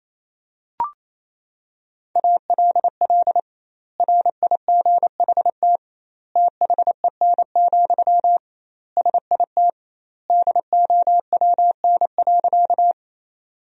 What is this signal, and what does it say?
Morse code, heard as a keyed tone, about 28 wpm, 700 Hz. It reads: ALL RIGHT THEN, SIT DOWN.
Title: 2x Word-Spacing - Sentences from Top 500 Words - 28wpm